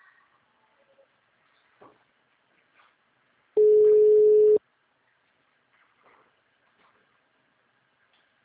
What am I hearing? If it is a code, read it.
Ringback tone (Europe, 425 Hz).